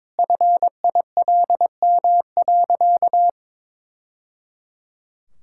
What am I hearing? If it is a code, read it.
Morse code, heard as a keyed tone, about 22 wpm, 700 Hz. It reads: FILM.
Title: Sentences from Top 1000 Words - Review Part 5 - 22wpm